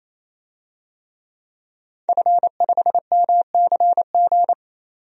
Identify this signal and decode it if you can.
Morse code, heard as a keyed tone, about 28 wpm, 700 Hz. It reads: F5MCG
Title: DXCC-100 Call Signs - 5 Characters - 28wpm